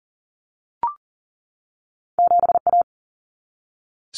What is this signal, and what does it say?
Morse code, heard as a keyed tone, about 41 wpm, 700 Hz. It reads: 7A
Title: Common Cut Numbers - 2-Digit 40wpm